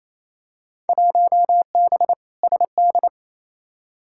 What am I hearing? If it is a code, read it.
Morse code, heard as a keyed tone, about 28 wpm, 700 Hz. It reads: 1B SD
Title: Field Day Exchanges Encore - 28wpm